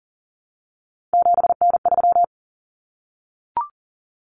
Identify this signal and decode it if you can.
Morse code, heard as a keyed tone, about 40 wpm, 700 Hz. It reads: 7N3
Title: Common Cut Numbers - 3-Digit 40wpm